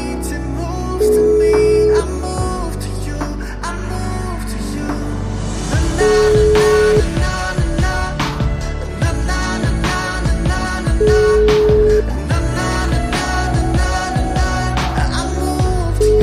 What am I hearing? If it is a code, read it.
Ringback tone (Europe, 425 Hz), with noise.